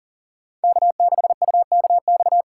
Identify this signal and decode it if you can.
Morse code, heard as a keyed tone, about 39 wpm, 700 Hz. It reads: K6UKX
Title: USA Call Signs - 5 Characters - 2x Character Spacing - 40wpm